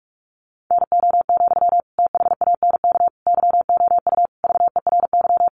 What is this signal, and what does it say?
Morse code, heard as a keyed tone, about 44 wpm, 700 Hz. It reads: NO, THANK YOU VERY
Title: Sentences from Top 600 Words 45wpm